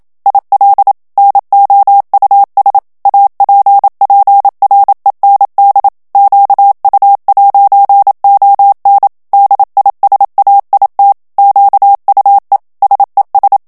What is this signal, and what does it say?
Morse code, heard as a keyed tone, about 27 wpm, 800 Hz. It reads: IL NOUS APPREND QU'ON DISAIT QUE SES
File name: FRANCAIS TEXTE CLAIR 2 VITESSE 26 MOTS MINUTES